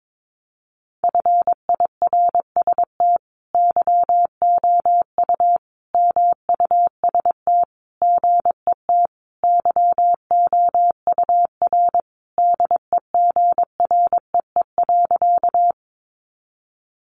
Morse code, heard as a keyed tone, about 22 wpm, 700 Hz. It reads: FIRST YOU MUST GET YOUR DEGREE.
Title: Sentences from Top 1500 Words 22wpm